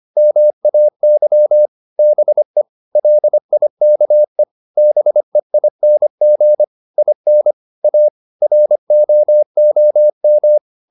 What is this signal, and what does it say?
Morse code, heard as a keyed tone, about 25 wpm, 600 Hz. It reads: MAY BE LIKE BEING IN A ROOM